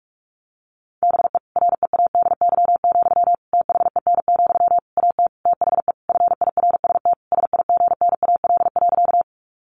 Morse code, heard as a keyed tone, about 45 wpm, 700 Hz. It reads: BE READY, THEN, AT THE FIRST SIGNAL.
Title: Sentences from Top 1400 Words 45wpm